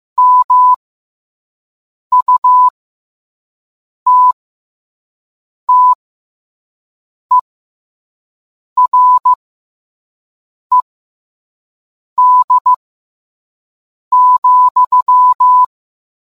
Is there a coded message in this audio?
Yes: Morse code from 0.2 to 15.6 s, heard as a keyed tone, about 15 wpm, spaced as for 5 wpm, 1000 Hz. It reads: MUTTERED,